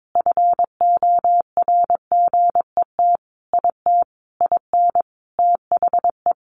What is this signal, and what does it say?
Morse code, heard as a keyed tone, about 22 wpm, 700 Hz. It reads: FORGET IT IN THE
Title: Sentences from Top 1200 Words 22wpm